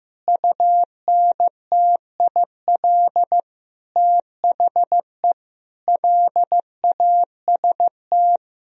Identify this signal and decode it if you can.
Morse code, heard as a keyed tone, about 16 wpm, 700 Hz. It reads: UNTIL THE LAST
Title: Sentences from Top 1000 Words - Review Part 1 - 15wpm